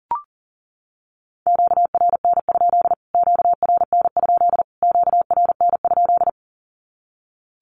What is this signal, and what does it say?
Morse code, heard as a keyed tone, about 40 wpm, 700 Hz. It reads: QRN? QRN? QRN?
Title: Q-Codes Repeated 3x 40wpm